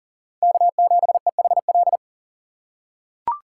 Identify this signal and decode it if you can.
Morse code, heard as a keyed tone, about 40 wpm, 700 Hz. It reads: K7EHL
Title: USA Call Signs - 5 Characters - 3x Character Spacing - 40wpm